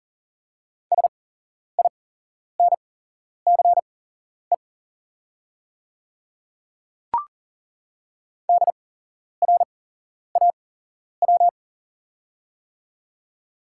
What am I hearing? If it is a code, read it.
Morse code, heard as a keyed tone, about 40 wpm, spaced as for 11 wpm, 700 Hz. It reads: SINCE DRAW
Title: 8x Character Spacing - Top 500 Words - 40wpm